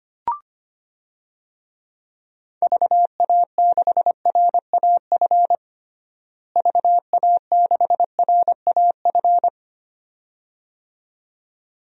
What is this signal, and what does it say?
Morse code, heard as a keyed tone, about 25 wpm, 700 Hz. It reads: VA6RAF VA6RAF
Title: DXCC-100 Call Signs - 6 Characters - Repeated 2x - 25wpm